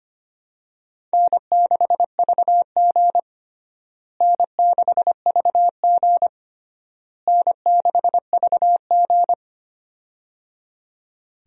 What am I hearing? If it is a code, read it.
Morse code, heard as a keyed tone, about 25 wpm, 700 Hz. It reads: N6VG N6VG N6VG
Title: USA Call Signs - 4 Characters - Repeated 3x - 25wpm